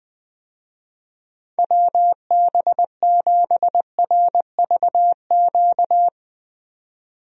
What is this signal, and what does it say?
Morse code, heard as a keyed tone, about 20 wpm, 700 Hz. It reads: WB7RVQ
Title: USA Call Signs - 6 Characters - 20wpm